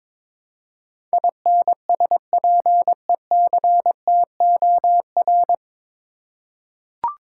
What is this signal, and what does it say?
Morse code, heard as a keyed tone, about 22 wpm, 700 Hz. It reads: INSPECTOR
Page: Morse Code Ninja practice 22wpm